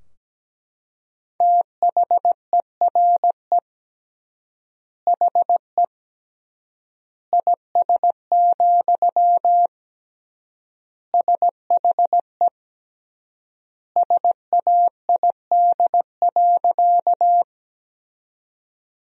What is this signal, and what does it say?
Morse code, heard as a keyed tone, about 17 wpm, 700 Hz. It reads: THERE HE IS, SHE SAID.
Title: 3x Word-Spacing - Sentences from Top 100 Words - 17wpm